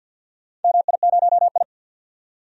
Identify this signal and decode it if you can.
Morse code, heard as a keyed tone, about 52 wpm, 700 Hz. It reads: MI0I
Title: DXCC-100 Call Signs - 4 Characters - 50wpm